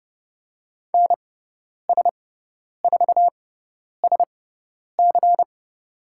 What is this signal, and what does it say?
Morse code, heard as a keyed tone, about 30 wpm, spaced as for 10 wpm, 700 Hz. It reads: NS4SC
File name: Lesson 16g - Letter c - Call Signs 30wpm